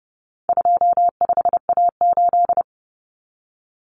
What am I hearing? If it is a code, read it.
Morse code, heard as a keyed tone, about 30 wpm, 700 Hz. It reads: 25A8
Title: Common Cut Numbers - 4-Digit 30wpm